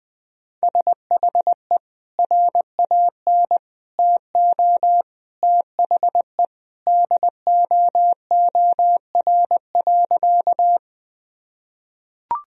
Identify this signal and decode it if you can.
Morse code, heard as a keyed tone, about 20 wpm, 700 Hz. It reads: SHE RAN TO THE DOOR.
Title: Sentences from Top 1000 Words - Review Part 3 - 20wpm